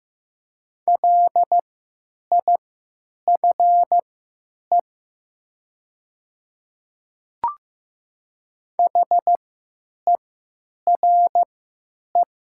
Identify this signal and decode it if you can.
Morse code, heard as a keyed tone, about 15 wpm, spaced as for 9 wpm, 700 Hz. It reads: LIFE HERE
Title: 3x Character Spacing - Top 300 Words - 15wpm